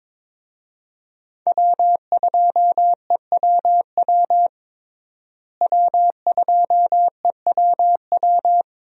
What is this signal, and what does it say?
Morse code, heard as a keyed tone, about 22 wpm, 700 Hz. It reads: W2EWW W2EWW
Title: USA Call Signs - 5 Characters - Repeated 2x - 22wpm